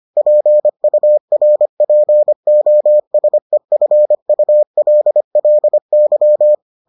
Morse code, heard as a keyed tone, about 25 wpm, 600 Hz. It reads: PURPOSEFULLY